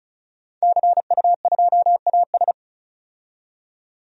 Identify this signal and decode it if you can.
Morse code, heard as a keyed tone, about 36 wpm, 700 Hz. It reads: CU2AS
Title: DXCC-100 Call Signs - 5 Characters - 35wpm